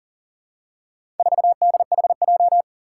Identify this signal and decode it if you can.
Morse code, heard as a keyed tone, about 40 wpm, 700 Hz. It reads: 4DHJ
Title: Four Challenging Letter-Number Combinations 40wpm